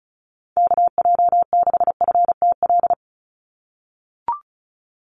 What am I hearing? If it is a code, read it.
Morse code, heard as a keyed tone, about 34 wpm, 700 Hz. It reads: KJ6FTL
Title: USA Call Signs 35wpm